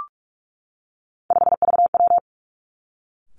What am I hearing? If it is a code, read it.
Morse code, heard as a keyed tone, about 46 wpm, 700 Hz. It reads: 5VW
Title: Three Challenging Letter-Number Combinations 45wpm